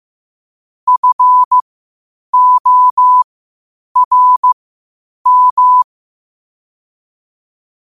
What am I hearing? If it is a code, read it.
Morse code, heard as a keyed tone, about 15 wpm, spaced as for 8 wpm, 1000 Hz. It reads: FORM
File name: qotdfiles / 8wpm